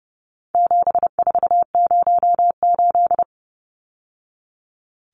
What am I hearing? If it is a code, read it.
Morse code, heard as a keyed tone, about 30 wpm, 700 Hz. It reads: 7408
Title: Numbers - 4-Digit 30wpm